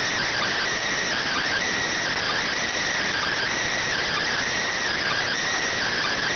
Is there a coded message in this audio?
Yes: Emergency Alert System (SAME) tones from 1.2 to 5.3 s, heard as harsh buzzing bursts.